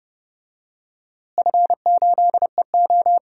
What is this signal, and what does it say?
Morse code, heard as a keyed tone, about 30 wpm, 700 Hz. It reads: F8EO
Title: DXCC-100 Call Signs - 4 Characters - Repeated 3x - 30wpm